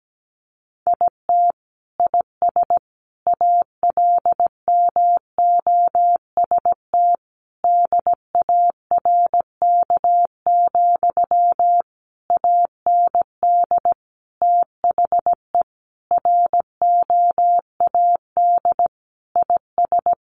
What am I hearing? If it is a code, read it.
Morse code, heard as a keyed tone, about 17 wpm, 700 Hz. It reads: IT IS ALMOST DARK, AND THE ROAD IS
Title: Sentences from Top 1200 Words 17wpm